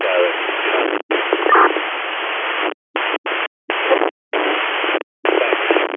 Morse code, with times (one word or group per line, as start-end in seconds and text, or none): none